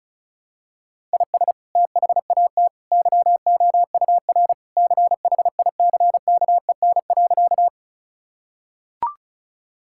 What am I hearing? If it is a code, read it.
Morse code, heard as a keyed tone, about 34 wpm, 700 Hz. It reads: IS THAT YOUR CHICKEN.
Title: Sentences from Top 1000 Words - Review Part 5 - 35wpm